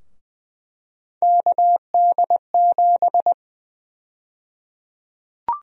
Morse code, heard as a keyed tone, about 20 wpm, 700 Hz. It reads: KD7
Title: USA Call Sign Prefixes - 20wpm